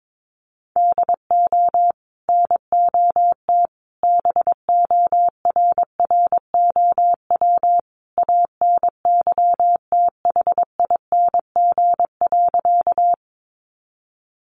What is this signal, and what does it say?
Morse code, heard as a keyed tone, about 22 wpm, 700 Hz. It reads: DO NOT BORROW ANYTHING.
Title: Sentences from Top 1500 Words 22wpm